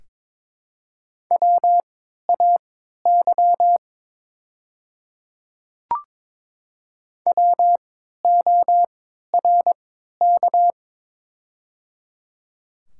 Morse code, heard as a keyed tone, about 22 wpm, spaced as for 13 wpm, 700 Hz. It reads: WAY WORK
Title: 3x Character Spacing - Top 200 Words - 22wpm